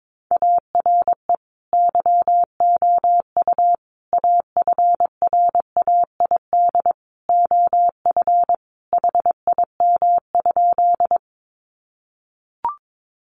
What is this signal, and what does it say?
Morse code, heard as a keyed tone, about 22 wpm, 700 Hz. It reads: ARE YOU AFRAID OF HIM?
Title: Sentences from Top 1500 Words 22wpm